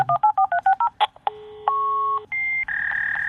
DTMF, heard as telephone touch tones, tones of about 90 ms. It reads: 1430